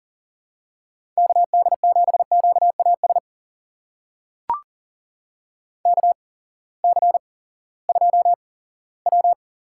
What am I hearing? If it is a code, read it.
Morse code, heard as a keyed tone, about 40 wpm, spaced as for 20 wpm, 700 Hz. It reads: KD7QAS KC2W
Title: USA Call Signs - 6 Characters - 8x Character Spacing - 40wpm